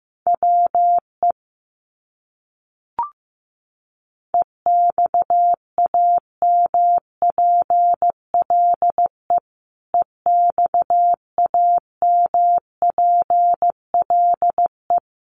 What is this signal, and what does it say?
Morse code, heard as a keyed tone, about 15 wpm, 700 Hz. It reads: WE EXAMPLE EXAMPLE
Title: Top 300 Words Repeated 15wpm